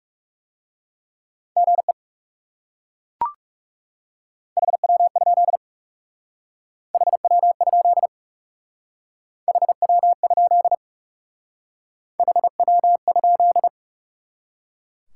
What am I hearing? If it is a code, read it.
Morse code, heard as a keyed tone, about 38 wpm, 700 Hz. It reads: GE HW? HW? HW? HW?